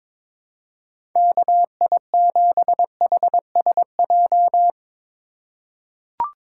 Morse code, heard as a keyed tone, about 22 wpm, 700 Hz. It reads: KI7HSJ
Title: USA Call Signs - 6 Characters - 4x Character Spacing - 22wpm